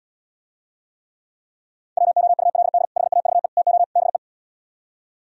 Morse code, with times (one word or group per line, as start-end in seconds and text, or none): 2.0-2.9 WOULD
3.0-3.5 HAVE
3.6-3.8 TO
4.0-4.2 BE